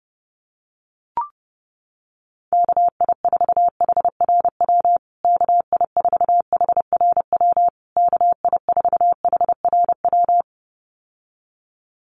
Morse code, heard as a keyed tone, about 30 wpm, 700 Hz. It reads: KI4HRW KI4HRW KI4HRW
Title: USA Call Signs Repeated 3x 30wpm